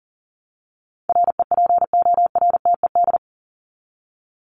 Morse code, heard as a keyed tone, about 40 wpm, 700 Hz. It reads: REPORTED
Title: ED Suffix - Top 100 Words with Base Words 40wpm